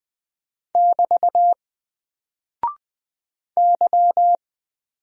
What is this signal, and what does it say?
Morse code, heard as a keyed tone, about 20 wpm, 700 Hz. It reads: <BT> Y
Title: Single Character (Letter:Number:Punctuation:Prosign) 20wpm